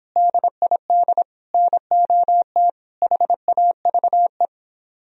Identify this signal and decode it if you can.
Morse code, heard as a keyed tone, about 31 wpm, 700 Hz. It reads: DID NOT HAVE